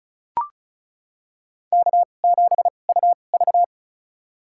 Morse code, heard as a keyed tone, about 36 wpm, spaced as for 25 wpm, 700 Hz. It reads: K7UV
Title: USA Call Signs - 4 Characters - 2x Character Spacing - 35wpm